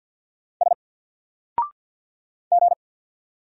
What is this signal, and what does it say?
Morse code, heard as a keyed tone, about 48 wpm, 700 Hz. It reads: S G
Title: Single Character (Letter:Number:Punctuation:Prosign) Alt 50wpm